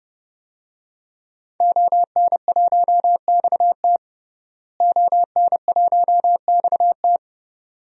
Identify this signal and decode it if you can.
Morse code, heard as a keyed tone, about 30 wpm, 700 Hz. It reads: ON1XT ON1XT
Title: DXCC-100 Call Signs - 5 Characters - Repeated 2x - 30wpm